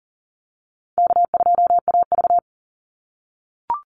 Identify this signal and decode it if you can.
Morse code, heard as a keyed tone, about 40 wpm, 700 Hz. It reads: K2AV
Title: Easy Call Signs - 40wpm